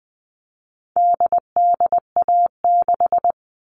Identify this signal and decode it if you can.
Morse code, heard as a keyed tone, about 20 wpm, 700 Hz. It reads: DDA6
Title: All Cut Numbers - 4-Digit 20wpm